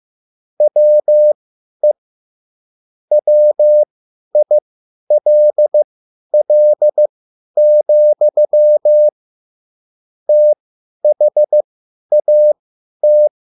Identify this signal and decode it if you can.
Morse code, heard as a keyed tone, about 15 wpm, spaced as for 10 wpm, 600 Hz. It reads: WE WILL, THAT